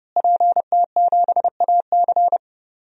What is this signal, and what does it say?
Morse code, heard as a keyed tone, about 30 wpm, 700 Hz. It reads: PT7AC
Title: DXCC-100 Call Signs - 5 Characters - Repeated 3x - 30wpm